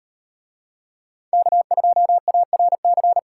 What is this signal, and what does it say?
Morse code, heard as a keyed tone, about 37 wpm, 700 Hz. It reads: K2ARC